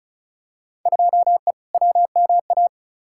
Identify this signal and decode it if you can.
Morse code, heard as a keyed tone, about 34 wpm, 700 Hz. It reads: 2E WMA
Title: Field Day Exchanges Encore - 35wpm